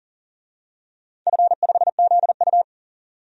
Morse code, heard as a keyed tone, about 41 wpm, 700 Hz. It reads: F5ZU
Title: DXCC-100 Call Signs - 4 Characters - 40wpm